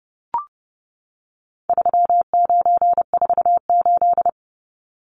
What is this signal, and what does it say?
Morse code, heard as a keyed tone, about 30 wpm, 700 Hz. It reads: 3948
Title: Numbers - 4-Digit 30wpm